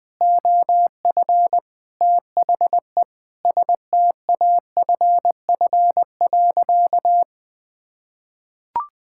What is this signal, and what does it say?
Morse code, heard as a keyed tone, about 21 wpm, 700 Hz. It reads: OF THE STAFF.
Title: Sentences with 5-Word Phrases - 20wpm